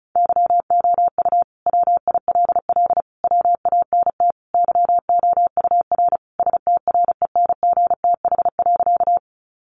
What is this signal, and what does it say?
Morse code, heard as a keyed tone, about 34 wpm, 700 Hz. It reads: YOU WILL WANT YOUR STRENGTH.
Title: Sentences from Top 1300 Words 35wpm